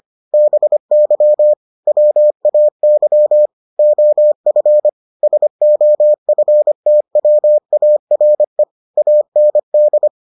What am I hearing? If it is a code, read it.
Morse code, heard as a keyed tone, about 25 wpm, 600 Hz. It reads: BY WAY OF SOFTWARE AND